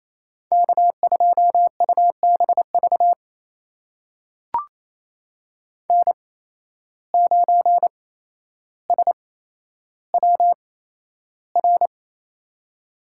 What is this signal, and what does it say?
Morse code, heard as a keyed tone, about 28 wpm, spaced as for 12 wpm, 700 Hz. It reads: K2UBV N9SWR
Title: USA Call Signs - 5 Characters - 8x Character Spacing - 28wpm